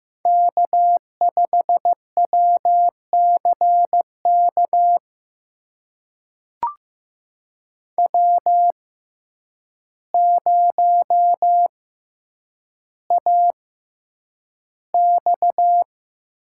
Morse code, heard as a keyed tone, about 15 wpm, spaced as for 8 wpm, 700 Hz. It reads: K5WCK W0AX
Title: USA Call Signs - 6x Character Spacing - 15wpm